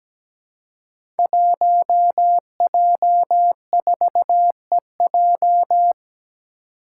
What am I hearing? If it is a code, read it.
Morse code, heard as a keyed tone, about 17 wpm, 700 Hz. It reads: 1J4EJ
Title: Five Challenging Letter-Number Combinations 17wpm